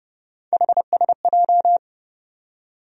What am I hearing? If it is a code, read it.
Morse code, heard as a keyed tone, about 30 wpm, 700 Hz. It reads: HSJ
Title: USA Call Sign Postfixes - 3 Letters 30wpm